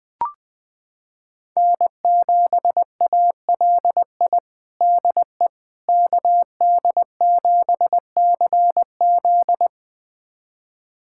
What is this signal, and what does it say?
Morse code, heard as a keyed tone, about 20 wpm, 700 Hz. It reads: N7ALI DE KD7CZ
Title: USA Callsign DE Callsign - 5 Characters - 20wpm